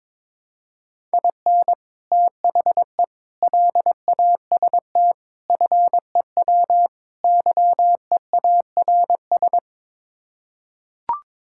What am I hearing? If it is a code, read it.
Morse code, heard as a keyed tone, about 22 wpm, 700 Hz. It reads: IN THE LAST FEW YEARS
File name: practice / 22wpm / Build A Phrase - 5-Word Phrases - 22wpm